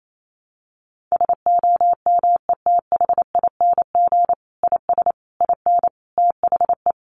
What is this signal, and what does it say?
Morse code, heard as a keyed tone, about 28 wpm, 700 Hz. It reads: SOMETHING IS IN THE
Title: Sentences from Top 600 Words 28wpm